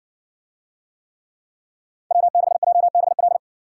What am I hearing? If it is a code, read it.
Morse code, heard as a keyed tone, about 54 wpm, 700 Hz. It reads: W6JBL